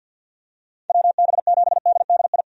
Morse code, heard as a keyed tone, about 52 wpm, 700 Hz. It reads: WB7DDI
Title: USA Call Signs - 6 Characters - 50wpm